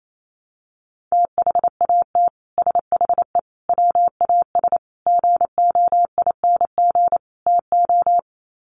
Morse code, heard as a keyed tone, about 28 wpm, 700 Hz. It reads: THAT SHE WAS GOING TO
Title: Sets of 5 Words Encore2 28wpm